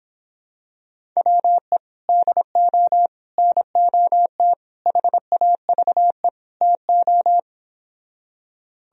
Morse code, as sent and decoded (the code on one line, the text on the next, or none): .-- . / -.. --- / -. --- - / .... .- ...- . / - ---
WE DO NOT HAVE TO